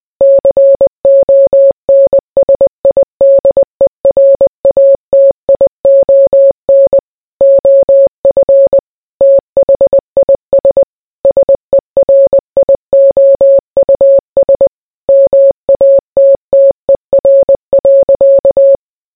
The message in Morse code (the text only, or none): CONSIDERATION OF THIS SERIOUS MATTER.